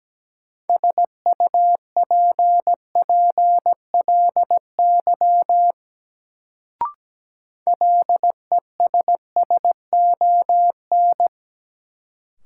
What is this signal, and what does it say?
Morse code, heard as a keyed tone, about 17 wpm, 700 Hz. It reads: SUPPLY LESSON
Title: Sentences from Top 1200 Words - New Word Drill Alt - 17wpm